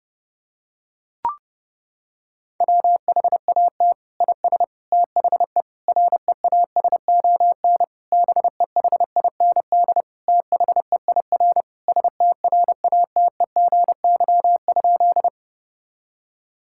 Morse code, heard as a keyed tone, about 32 wpm, 700 Hz. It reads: WHAT IS THE REASON BEHIND THEIR STRATEGY?
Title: Sentences from Top 1000 Words - Review Part 5 - 30wpm